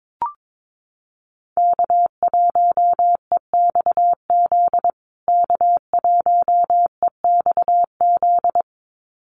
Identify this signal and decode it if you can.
Morse code, heard as a keyed tone, about 22 wpm, 700 Hz. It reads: K1EXZ K1EXZ
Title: USA Call Signs Repeated 2x 22wpm